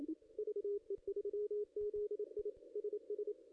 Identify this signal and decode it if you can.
Morse code, heard as a keyed tone, about 27 wpm, 410 Hz. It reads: VE3ZI SS
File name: contests / ss-cw-k5tr-op-n5rz -0400